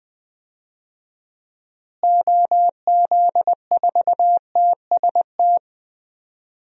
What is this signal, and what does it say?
Morse code, heard as a keyed tone, about 20 wpm, 700 Hz. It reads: OZ4TST
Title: DXCC-100 Call Signs - 6 Characters - Repeated 3x - 20wpm